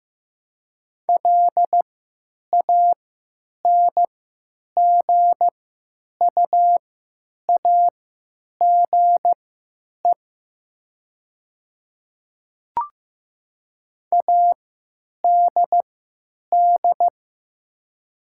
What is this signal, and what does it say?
Morse code, heard as a keyed tone, about 15 wpm, spaced as for 9 wpm, 700 Hz. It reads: LANGUAGE ADD
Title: 3x Character Spacing - Top 500 Words - 15wpm